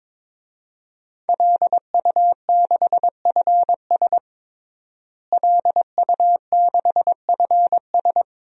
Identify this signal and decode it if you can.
Morse code, heard as a keyed tone, about 21 wpm, 700 Hz. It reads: LU6FS LU6FS
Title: DXCC-100 Call Signs - 5 Characters - Repeated 2x - 22wpm